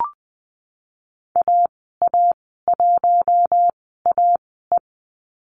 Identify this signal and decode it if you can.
Morse code, heard as a keyed tone, about 20 wpm, spaced as for 15 wpm, 700 Hz. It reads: AA1AE
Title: 042 Lesson 8g - Number 1 - Call Signs 20wpm